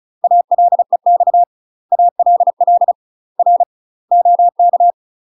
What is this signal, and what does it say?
Morse code, heard as a keyed tone, about 34 wpm, 700 Hz. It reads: ALEX ALL R OK
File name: UK MASTER QSOs TRAINING / 35WPM / UK MASTER QSOs 0003